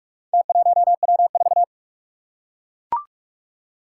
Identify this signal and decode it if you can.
Morse code, heard as a keyed tone, about 44 wpm, 700 Hz. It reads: T1W4
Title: Four Challenging Letter-Number Combinations 45wpm